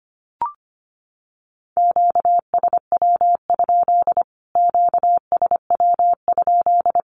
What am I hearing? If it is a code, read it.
Morse code, heard as a keyed tone, about 25 wpm, 700 Hz. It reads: QSW? QSW?
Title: Q-Codes Repeated 3x 25wpm